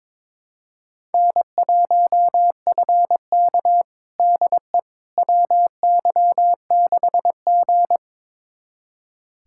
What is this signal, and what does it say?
Morse code, heard as a keyed tone, about 22 wpm, 700 Hz. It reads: N1FK DE WY6G
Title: USA Callsign DE Callsign - 4 Characters - 22wpm